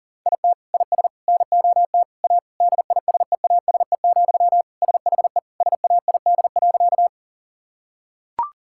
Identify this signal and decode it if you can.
Morse code, heard as a keyed tone, about 40 wpm, 700 Hz. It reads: IT IS NOT A DISEASE, SHE SAID.
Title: Sentences from Top 1000 Words - Review Part 6 - 40wpm